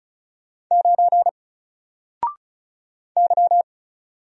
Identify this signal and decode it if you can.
Morse code, heard as a keyed tone, about 36 wpm, 700 Hz. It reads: 9 Y
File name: Single-Letter-Number-35wpm